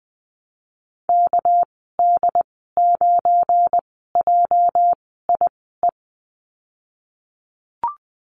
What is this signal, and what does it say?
Morse code, heard as a keyed tone, about 20 wpm, spaced as for 14 wpm, 700 Hz. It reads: KD9JIE
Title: 245 Lesson 39g - Letter z - Call Signs 20wpm